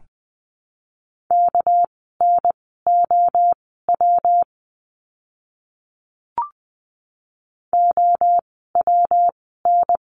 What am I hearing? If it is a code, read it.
Morse code, heard as a keyed tone, about 20 wpm, spaced as for 14 wpm, 700 Hz. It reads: KNOW OWN
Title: 237 Lesson 38f - Letter q - Words 20wpm